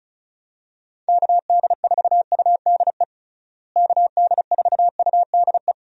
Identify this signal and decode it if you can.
Morse code, heard as a keyed tone, about 36 wpm, 700 Hz. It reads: KD4UDE KD4UDE
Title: USA Call Signs - 6 Characters - Repeated 3x - 35wpm